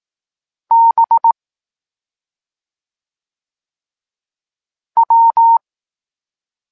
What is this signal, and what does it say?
Morse code, heard as a keyed tone, about 18 wpm, 920 Hz. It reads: B W
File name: Tekst-23 4